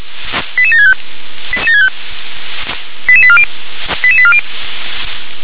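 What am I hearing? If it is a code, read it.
Five-tone selective call (ZVEI), heard as quick runs of five tones, tones of about 70 ms. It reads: 88765 88840 88840
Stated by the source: ZVEI five-tone selective call received on SDR.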